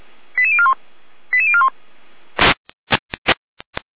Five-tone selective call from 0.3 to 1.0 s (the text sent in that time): 88041